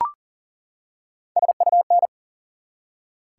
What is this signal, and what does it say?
Morse code, heard as a keyed tone, about 38 wpm, 700 Hz. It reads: SUN
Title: Three Letter Words - 40wpm